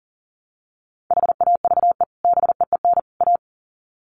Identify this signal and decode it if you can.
Morse code, heard as a keyed tone, about 40 wpm, 700 Hz. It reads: HAVE BEEN A
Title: Sets of 3 Words 40wpm